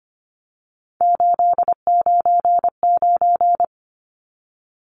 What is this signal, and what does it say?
Morse code, heard as a keyed tone, about 25 wpm, 700 Hz. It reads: 899
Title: Numbers - 3-Digit 25wpm